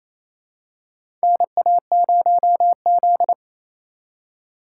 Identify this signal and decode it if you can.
Morse code, heard as a keyed tone, about 28 wpm, 700 Hz. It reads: NA0Z
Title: USA Call Signs - 4 Characters - 2x Character Spacing - 28wpm